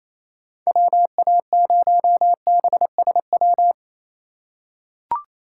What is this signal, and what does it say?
Morse code, heard as a keyed tone, about 27 wpm, 700 Hz. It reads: WA0BSW
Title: USA Call Signs - 6 Characters - 3x Character Spacing - 28wpm